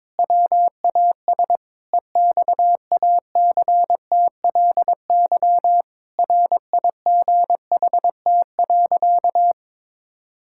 Morse code, heard as a keyed tone, about 22 wpm, 700 Hz. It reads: WAS EXACTLY RIGHT.
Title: Sentences from Top 1000 Words - Review Part 6 - 22wpm